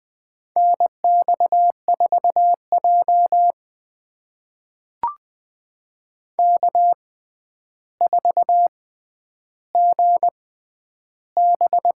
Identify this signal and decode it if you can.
Morse code, heard as a keyed tone, about 20 wpm, spaced as for 10 wpm, 700 Hz. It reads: NX4J K4GB
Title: USA Call Signs - 4 Characters - 6x Character Spacing - 20wpm